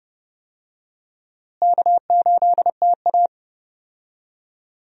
Morse code, heard as a keyed tone, about 30 wpm, 700 Hz. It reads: K8TA
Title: USA Call Signs - 4 Characters - Repeated 2x - 30wpm